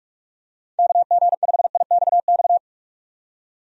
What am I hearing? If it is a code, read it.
Morse code, heard as a keyed tone, about 47 wpm, 700 Hz. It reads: KG5IXX
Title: USA Call Signs - 6 Characters - 45wpm